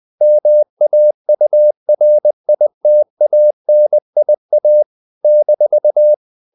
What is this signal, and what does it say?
Morse code, heard as a keyed tone, about 20 wpm, 600 Hz. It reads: MAURITANIA -
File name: CW / Practice / NationalCapitals20